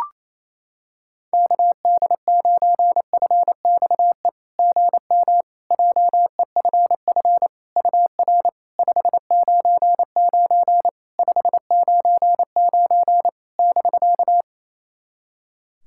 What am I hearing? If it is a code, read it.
Morse code, heard as a keyed tone, about 28 wpm, 700 Hz. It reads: KD9FXE GM JEFF UR 599 599 <BK>
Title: SOTA QSO Practice 28wpm